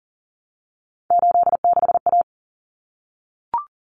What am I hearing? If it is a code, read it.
Morse code, heard as a keyed tone, about 40 wpm, 700 Hz. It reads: 86A
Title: Common Cut Numbers - 3-Digit 40wpm